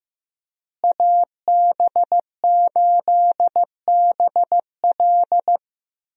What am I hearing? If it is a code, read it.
Morse code, heard as a keyed tone, about 15 wpm, 700 Hz. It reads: AB8BL
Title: USA Call Signs - 5 Characters - Repeated 3x - 15wpm